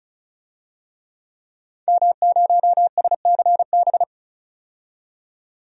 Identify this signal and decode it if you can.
Morse code, heard as a keyed tone, about 34 wpm, 700 Hz. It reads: M0SCB
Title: DXCC-100 Call Signs - 5 Characters - 35wpm